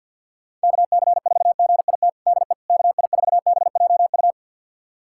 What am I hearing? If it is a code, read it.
Morse code, heard as a keyed tone, about 50 wpm, 700 Hz. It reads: KC4GIT DE KI4BJU
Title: USA Callsign DE Callsign 50wpm